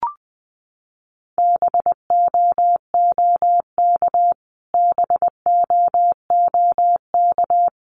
Morse code, heard as a keyed tone, about 20 wpm, 700 Hz. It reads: BOOK BOOK
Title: Top 300 Words Repeated 20wpm